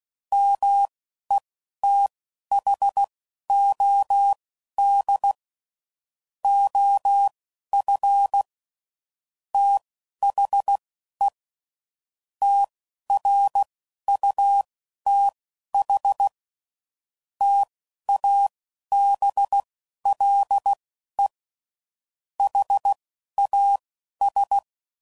Morse code, heard as a keyed tone, about 16 wpm, spaced as for 11 wpm, 800 Hz. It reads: METHOD OF THE TRUTH TABLE HAS
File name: CW Code Groups @10WPM/Technical Subjects